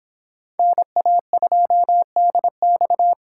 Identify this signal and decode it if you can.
Morse code, heard as a keyed tone, about 30 wpm, 700 Hz. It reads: NA2DX